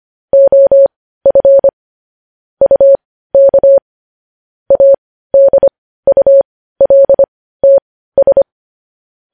Morse code, heard as a keyed tone, about 25 wpm, spaced as for 15 wpm, 550 Hz. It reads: OF UK ADULTS
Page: news headlines in morse code at 15 wpm